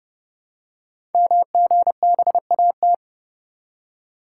Morse code, heard as a keyed tone, about 30 wpm, 700 Hz. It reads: MGBAT
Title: Code Groups - 5 Letter:Numbers - 30wpm